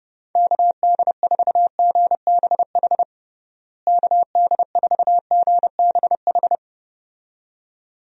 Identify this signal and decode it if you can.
Morse code, heard as a keyed tone, about 30 wpm, 700 Hz. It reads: KD4GBH KD4GBH
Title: USA Call Signs - 6 Characters - Repeated 2x - 30wpm